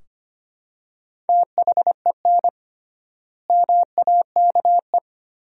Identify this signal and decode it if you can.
Morse code, heard as a keyed tone, about 25 wpm, 700 Hz. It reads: THEN MAKE
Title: 3x Word-Spacing - Sentences from Top 100 Words - 25wpm